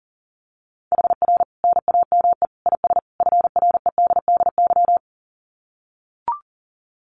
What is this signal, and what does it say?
Morse code, heard as a keyed tone, about 40 wpm, 700 Hz. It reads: HR NAME IS FREDDY
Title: Common QSO Phrases 40wpm